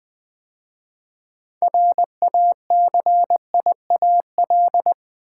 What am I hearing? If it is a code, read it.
Morse code, heard as a keyed tone, about 20 wpm, 700 Hz. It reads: RACIAL
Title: 6 Letter Words - First 3 Letters Given - 20wpm